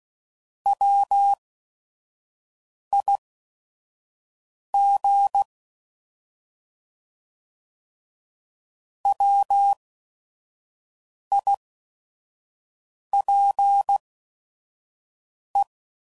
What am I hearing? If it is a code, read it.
Morse code, heard as a keyed tone, about 16 wpm, spaced as for 5 wpm, 800 Hz. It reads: WIG WIPE
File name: Prefixes / By First Letter / Ws